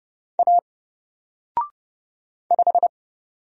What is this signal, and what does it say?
Morse code, heard as a keyed tone, about 30 wpm, 700 Hz. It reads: A 5
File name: Single-Letter-Number-30wpm